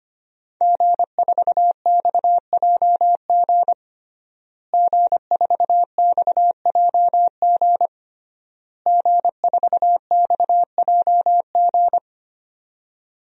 Morse code, heard as a keyed tone, about 25 wpm, 700 Hz. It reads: G4XJG G4XJG G4XJG
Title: DXCC-100 Call Signs - 5 Characters - Repeated 3x - 25wpm